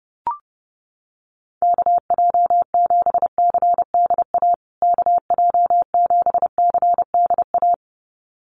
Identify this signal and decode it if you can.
Morse code, heard as a keyed tone, about 28 wpm, 700 Hz. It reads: KJ7CDA KJ7CDA
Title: USA Call Signs Repeated 2x 30wpm